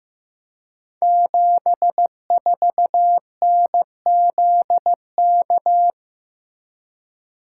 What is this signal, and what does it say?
Morse code, heard as a keyed tone, about 15 wpm, 700 Hz. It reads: 74NZK
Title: Code Groups - 5 Letter:Numbers - 15wpm